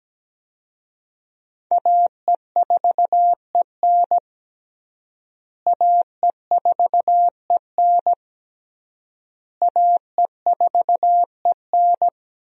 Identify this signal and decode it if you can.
Morse code, heard as a keyed tone, about 17 wpm, 700 Hz. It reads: AE4EN AE4EN AE4EN
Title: USA Call Signs - 5 Characters - Repeated 3x - 17wpm